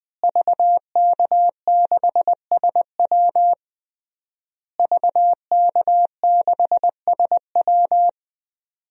Morse code, heard as a keyed tone, about 21 wpm, 700 Hz. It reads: VK6SW VK6SW
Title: DXCC-100 Call Signs - 5 Characters - Repeated 3x - 20wpm